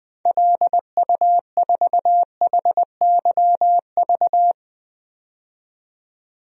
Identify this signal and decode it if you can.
Morse code, heard as a keyed tone, about 20 wpm, 700 Hz. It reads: LU4HYV
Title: DXCC-100 Call Signs - 6 Characters - 20wpm